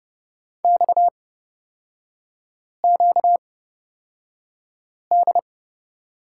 Morse code, heard as a keyed tone, about 30 wpm, spaced as for 5 wpm, 700 Hz. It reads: XQD
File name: Lesson 38e - Letter q - Three Characters 30wpm